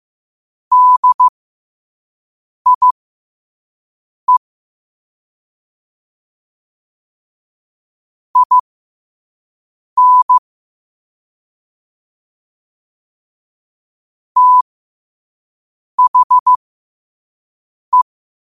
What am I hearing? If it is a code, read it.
Morse code, heard as a keyed tone, about 15 wpm, spaced as for 5 wpm, 1000 Hz. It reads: DIE IN THE
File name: 5wpm